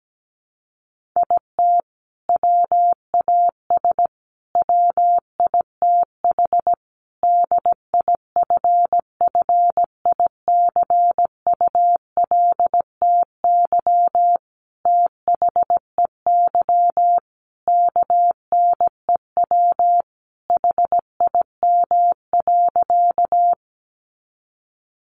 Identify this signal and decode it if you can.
Morse code, heard as a keyed tone, about 17 wpm, 700 Hz. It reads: IT WAS WITH DIFFICULTY THEY KNEW HIM.
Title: Sentences from Top 1500 Words 17wpm